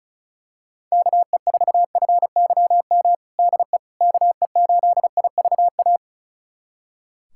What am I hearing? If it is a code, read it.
Morse code, heard as a keyed tone, about 34 wpm, 700 Hz. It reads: KE4FYM DE KE8IVA
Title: USA Callsign DE Callsign - 6 Characters - 35wpm